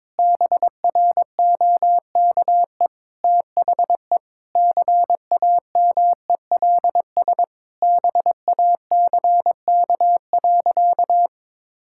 Morse code, heard as a keyed tone, about 22 wpm, 700 Hz. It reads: BROKE THE CAMELS BACK.
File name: practice / 22wpm / Sentences with 8-Word Phrases - 22wpm